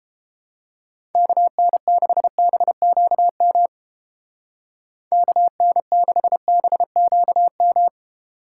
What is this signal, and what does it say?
Morse code, heard as a keyed tone, about 30 wpm, 700 Hz. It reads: KN6BQM KN6BQM